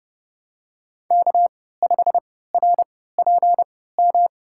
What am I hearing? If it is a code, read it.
Morse code, heard as a keyed tone, about 32 wpm, spaced as for 17 wpm, 700 Hz. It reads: K5RPM
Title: USA Call Signs - 3x Character Spacing - 30wpm